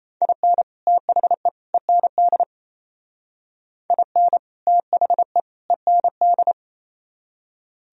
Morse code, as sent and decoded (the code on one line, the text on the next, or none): .. -. / - .... / . -. -.. / .. -. / - .... / . -. -..
IN THE END IN THE END